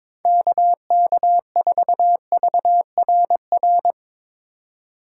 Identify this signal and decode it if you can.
Morse code, heard as a keyed tone, about 23 wpm, 700 Hz. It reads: KK4VRR
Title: USA Call Signs - 3x Character Spacing - 22wpm